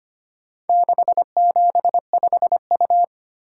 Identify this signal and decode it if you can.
Morse code, heard as a keyed tone, about 25 wpm, 700 Hz. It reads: 675U
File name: practice / 25wpm / Four Letter-Number Combinations - 25wpm